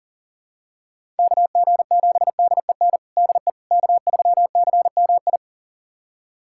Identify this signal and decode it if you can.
Morse code, heard as a keyed tone, about 40 wpm, 700 Hz. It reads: KG7DEN DE K3CMI
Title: USA Callsign DE Callsign 40wpm